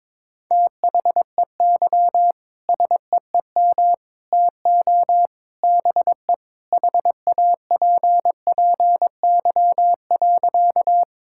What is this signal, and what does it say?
Morse code, heard as a keyed tone, about 22 wpm, 700 Hz. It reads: THEY SEEM TO BE HAPPY.
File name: practice / 22wpm / Sentences with 3-Word Phrases - 22wpm